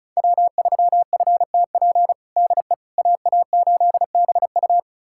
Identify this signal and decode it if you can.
Morse code, heard as a keyed tone, about 34 wpm, 700 Hz. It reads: W3FTP DE AA8BU
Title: USA Callsign DE Callsign - 5 Characters - 35wpm